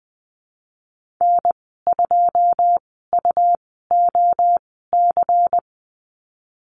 Morse code, heard as a keyed tone, about 20 wpm, spaced as for 14 wpm, 700 Hz. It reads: N2UOC
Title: 140 Lesson 23g - Letter f - Call Signs 20wpm